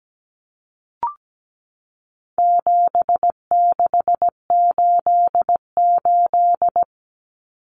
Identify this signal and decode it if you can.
Morse code, heard as a keyed tone, about 17 wpm, 700 Hz. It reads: 7688
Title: Common Cut Numbers - 4-Digit 17wpm